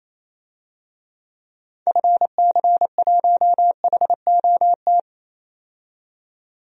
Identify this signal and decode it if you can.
Morse code, heard as a keyed tone, about 28 wpm, 700 Hz. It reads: FC1HOT
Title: DXCC-100 Call Signs - 6 Characters - 28wpm